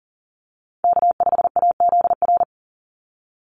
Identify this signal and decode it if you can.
Morse code, heard as a keyed tone, about 40 wpm, 700 Hz. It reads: K5AZR
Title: Call Signs - 40wpm